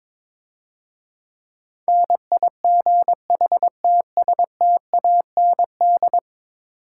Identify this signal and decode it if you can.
Morse code, heard as a keyed tone, about 22 wpm, 700 Hz. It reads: NIGHTSTAND
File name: Easy Words - Speed-Racing with Hints - 22wpm